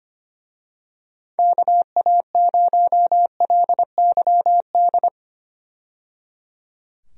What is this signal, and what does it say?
Morse code, heard as a keyed tone, about 25 wpm, 700 Hz. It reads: KA0LYD
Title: USA Call Signs - 6 Characters - 25wpm